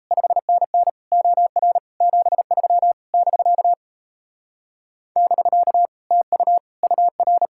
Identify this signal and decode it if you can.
Morse code, heard as a keyed tone, about 33 wpm, 700 Hz. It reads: 5NN OR 73 <BK> <BK> TU UR
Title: Speed-Racing Common QSO Phrases Encore - 28wpm